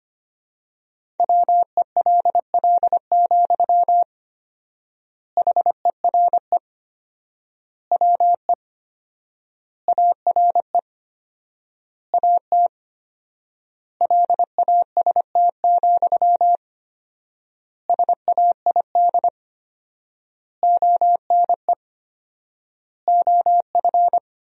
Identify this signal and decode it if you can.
Morse code, heard as a keyed tone, about 25 wpm, 700 Hz. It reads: WELL, HERE WE ARE AT LAST, SAID ONE OF
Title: 4x Word-Spacing - Sentences from Top 300 Words - 25wpm